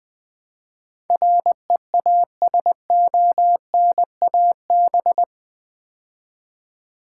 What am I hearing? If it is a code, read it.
Morse code, heard as a keyed tone, about 20 wpm, 700 Hz. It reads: REASONAB
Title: Build A Word - 4 Letters At A Time - 10-Letter Words - 20wpm